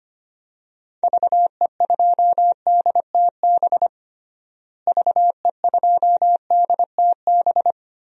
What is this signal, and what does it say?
Morse code, heard as a keyed tone, about 25 wpm, 700 Hz. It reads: VE2DTB VE2DTB
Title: DXCC-100 Call Signs - 6 Characters - Repeated 3x - 25wpm